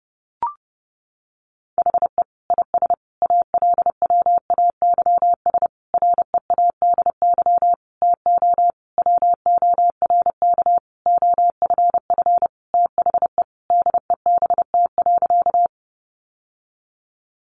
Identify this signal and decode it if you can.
Morse code, heard as a keyed tone, about 30 wpm, 700 Hz. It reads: HE IS ALWAYS READY TO WORK OFF THE DEBT.
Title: Sentences from Top 1300 Words 30wpm